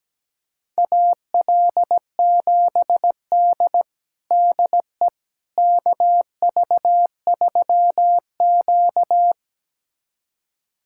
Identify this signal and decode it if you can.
Morse code, heard as a keyed tone, about 17 wpm, 700 Hz. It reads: AL7D DE KV3Q
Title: USA Callsign DE Callsign - 4 Characters - 17wpm